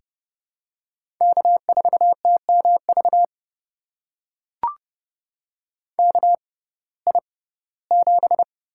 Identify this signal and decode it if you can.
Morse code, heard as a keyed tone, about 30 wpm, spaced as for 18 wpm, 700 Hz. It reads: K4TMV KI7
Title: USA Call Signs - 6x Character Spacing - 30wpm